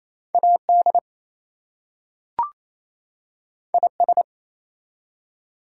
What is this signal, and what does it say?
Morse code, heard as a keyed tone, about 27 wpm, 700 Hz. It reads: AD IS
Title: Two Letter Words - 28wpm